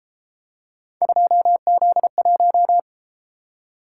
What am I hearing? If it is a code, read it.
Morse code, heard as a keyed tone, about 33 wpm, 700 Hz. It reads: 2Z1